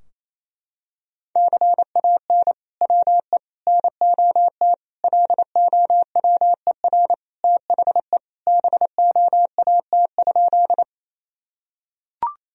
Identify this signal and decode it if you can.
Morse code, heard as a keyed tone, about 28 wpm, 700 Hz. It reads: CAN WE NOT LOWER THE BOAT?
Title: Sentences from Top 1000 Words - Review Part 4 - 28wpm